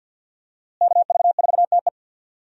Morse code, heard as a keyed tone, about 53 wpm, 700 Hz. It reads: KV4TE
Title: USA Call Signs - 5 Characters - Repeated 2x - 50wpm